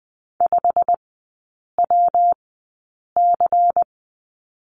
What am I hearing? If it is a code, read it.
Morse code, heard as a keyed tone, about 20 wpm, spaced as for 8 wpm, 700 Hz. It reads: 5WC
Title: 138 Lesson 23e - Letter f - Three Characters 20wpm